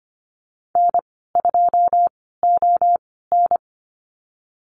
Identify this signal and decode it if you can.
Morse code, heard as a keyed tone, about 25 wpm, spaced as for 16 wpm, 700 Hz. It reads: N2ON
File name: Lesson 15g - Number 5 - Call Signs 25wpm